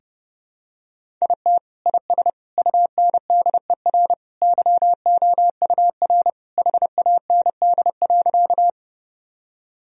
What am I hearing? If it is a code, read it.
Morse code, heard as a keyed tone, about 30 wpm, 700 Hz. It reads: IT IS UNDER YOUR HAND.
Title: Sentences from Top 1000 Words - Review Part 1 - 30wpm